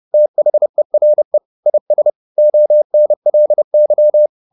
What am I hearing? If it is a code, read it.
Morse code, heard as a keyed tone, about 30 wpm, 600 Hz. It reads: THERE IS ONLY
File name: the old man and the sea / the old man and the sea 1 30wpm